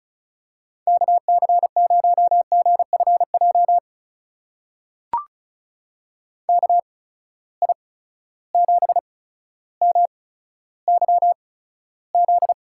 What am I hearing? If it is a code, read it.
Morse code, heard as a keyed tone, about 34 wpm, spaced as for 15 wpm, 700 Hz. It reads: KC0GFJ KI7MYZ
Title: USA Call Signs - 6 Characters - 8x Character Spacing - 35wpm